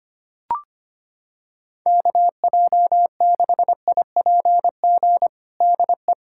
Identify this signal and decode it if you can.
Morse code, heard as a keyed tone, about 25 wpm, 700 Hz. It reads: KJ6IPG DE
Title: USA Callsign DE Callsign 25wpm